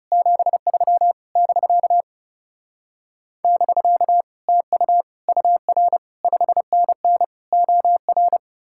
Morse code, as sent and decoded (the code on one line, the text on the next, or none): --... ...-- / -...-.- / -...-.- / - ..- / ..- .-. / ..... -. -. / --- .-.
73 <BK> <BK> TU UR 5NN OR